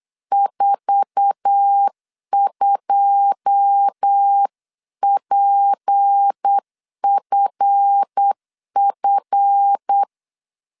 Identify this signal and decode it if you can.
Morse code, heard as a keyed tone, about 8 wpm, 800 Hz. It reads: 42PFF
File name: CW / DET / 1 9 1500C S A 8wpm 5637